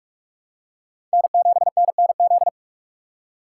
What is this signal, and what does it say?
Morse code, heard as a keyed tone, about 47 wpm, 700 Hz. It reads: N7NNZ
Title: USA Call Signs - 5 Characters - 6x Character Spacing - 45wpm